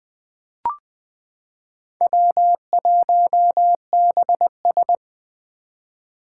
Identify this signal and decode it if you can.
Morse code, heard as a keyed tone, about 20 wpm, 700 Hz. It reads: W1BS
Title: Four Challenging Letter-Number Combinations 20wpm